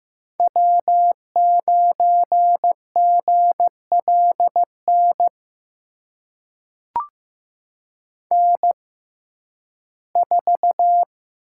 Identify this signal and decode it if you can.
Morse code, heard as a keyed tone, about 15 wpm, spaced as for 11 wpm, 700 Hz. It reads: W9GLN N4
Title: USA Call Signs - 5 Characters - 6x Character Spacing - 15wpm